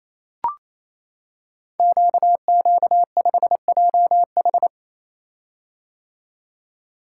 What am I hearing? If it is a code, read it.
Morse code, heard as a keyed tone, about 28 wpm, 700 Hz. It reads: QQ5JH
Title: Code Groups - 5 Letter:Numbers - 28wpm